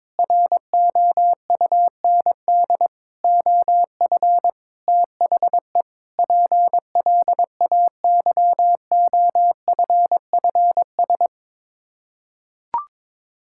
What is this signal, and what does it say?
Morse code, heard as a keyed tone, about 22 wpm, 700 Hz. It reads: ROUND OF THE PLAYOFFS
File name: practice / 22wpm / Sets of 7 Words - 22wpm